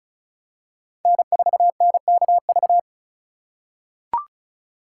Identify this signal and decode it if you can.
Morse code, heard as a keyed tone, about 34 wpm, 700 Hz. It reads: N4NKV
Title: USA Call Signs - 4x Character Spacing - 35wpm